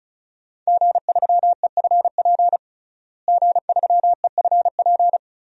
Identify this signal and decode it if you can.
Morse code, heard as a keyed tone, about 36 wpm, 700 Hz. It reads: G3EFP G3EFP
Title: DXCC-100 Call Signs - 5 Characters - Repeated 3x - 35wpm